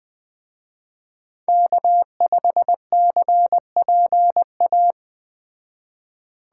Morse code, heard as a keyed tone, about 20 wpm, 700 Hz. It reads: K5CPA
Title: Speed Racing - USA Call Signs 17wpm